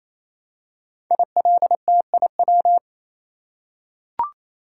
Morse code, heard as a keyed tone, about 27 wpm, 700 Hz. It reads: ILTIW
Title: Code Groups - 5 Letters - 28wpm